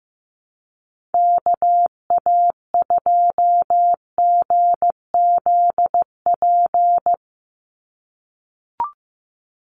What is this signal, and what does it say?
Morse code, heard as a keyed tone, about 14 wpm, 700 Hz. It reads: KA2GZP
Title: USA Call Signs 15wpm